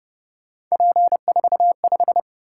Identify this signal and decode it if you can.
Morse code, heard as a keyed tone, about 29 wpm, 700 Hz. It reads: P45